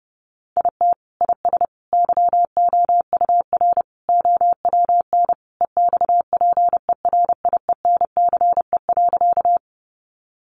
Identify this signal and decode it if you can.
Morse code, heard as a keyed tone, about 30 wpm, 700 Hz. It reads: IT IS YOUR OWN EXPERIENCE.
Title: Sentences from Top 600 Words 30wpm